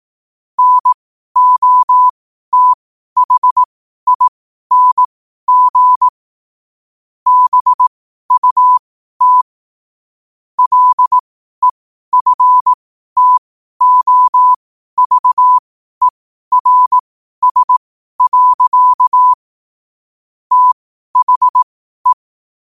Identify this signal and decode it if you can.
Morse code, heard as a keyed tone, about 18 wpm, spaced as for 12 wpm, 1000 Hz. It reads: NOTHING BUT LEFTOVERS. THE
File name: qotdfiles / 12wpm